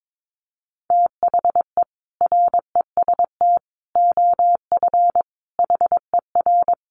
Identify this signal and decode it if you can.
Morse code, heard as a keyed tone, about 22 wpm, 700 Hz. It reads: THE REST OF HER
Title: Sets of 5 Words Encore 22wpm